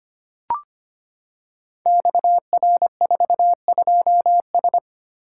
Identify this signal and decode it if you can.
Morse code, heard as a keyed tone, about 25 wpm, 700 Hz. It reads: XR42S
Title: Code Groups - 5 Letter:Numbers - 25wpm